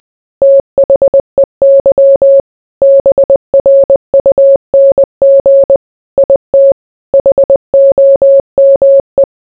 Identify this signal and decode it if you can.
Morse code, heard as a keyed tone, about 20 wpm, 550 Hz. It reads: THEY BRUNG IT HOME